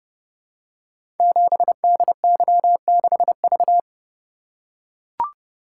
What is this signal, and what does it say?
Morse code, heard as a keyed tone, about 30 wpm, 700 Hz. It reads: 7DY6V
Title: Code Groups - 5 Letter:Numbers - 30wpm